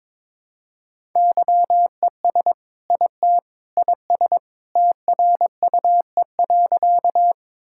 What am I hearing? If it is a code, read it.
Morse code, heard as a keyed tone, about 21 wpm, 700 Hz. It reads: YES IT IS TRUE.
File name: practice / 22wpm / Sentences with 2-Word Phrases - 22wpm